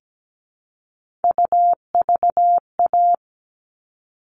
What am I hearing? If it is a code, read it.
Morse code, heard as a keyed tone, about 17 wpm, 700 Hz. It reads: UVA
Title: All Cut Numbers - 3-Digit 17wpm